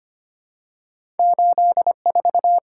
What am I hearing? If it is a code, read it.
Morse code, heard as a keyed tone, about 25 wpm, 700 Hz. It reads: 84
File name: Two Letter-Number Combinations - 25wpm